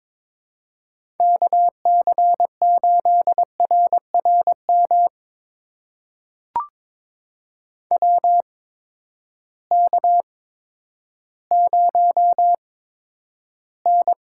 Morse code, heard as a keyed tone, about 23 wpm, spaced as for 11 wpm, 700 Hz. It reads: KC8RRM WK0N
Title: USA Call Signs - 8x Character Spacing - 22wpm